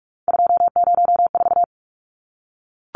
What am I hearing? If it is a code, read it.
Morse code, heard as a keyed tone, about 44 wpm, 700 Hz. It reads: 204
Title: Numbers - 3-Digit 45wpm